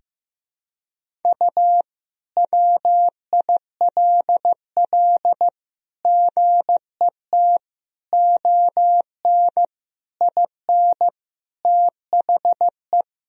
Morse code, heard as a keyed tone, about 15 wpm, 700 Hz. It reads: U WILL GET ON IN THE
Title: Sentences from Top 200 Words and Abbreviations 15wpm